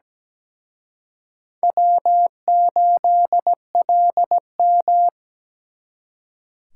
Morse code, heard as a keyed tone, about 20 wpm, 700 Hz. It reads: W8LM
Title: Speed Racing - USA 1x2 Call Signs 17wpm